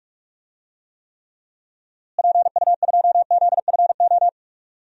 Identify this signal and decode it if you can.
Morse code, heard as a keyed tone, about 44 wpm, 700 Hz. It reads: PU2ZFO
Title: DXCC-100 Call Signs - 6 Characters - Repeated 2x - 45wpm